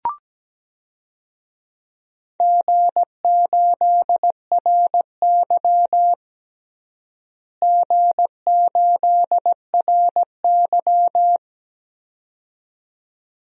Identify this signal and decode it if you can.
Morse code, heard as a keyed tone, about 17 wpm, 700 Hz. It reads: G8RY G8RY
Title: DXCC-100 Call Signs - 4 Characters - Repeated 2x - 17wpm